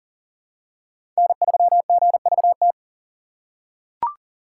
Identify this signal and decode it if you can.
Morse code, heard as a keyed tone, about 40 wpm, 700 Hz. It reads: N3GVT
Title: USA Call Signs - 5 Characters - 3x Character Spacing - 40wpm